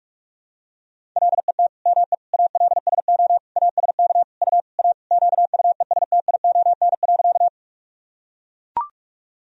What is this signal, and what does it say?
Morse code, heard as a keyed tone, about 41 wpm, 700 Hz. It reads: LET ME ALSO ASK U A QUESTION.
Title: Sentences from Top 400 Words and Abbreviations 45wpm